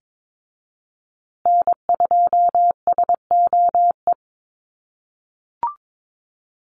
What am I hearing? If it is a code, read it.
Morse code, heard as a keyed tone, about 22 wpm, 700 Hz. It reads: N2SOE
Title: USA Call Signs Repeated 2x 22wpm